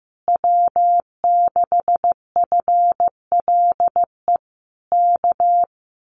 Morse code, heard as a keyed tone, about 15 wpm, 700 Hz. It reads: W6FLE K
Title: Common QSO Phrases Encore 15wpm